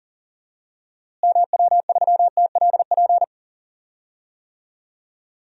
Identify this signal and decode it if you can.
Morse code, heard as a keyed tone, about 38 wpm, 700 Hz. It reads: MW3TLP
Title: DXCC-100 Call Signs - 6 Characters - Repeated 2x - 40wpm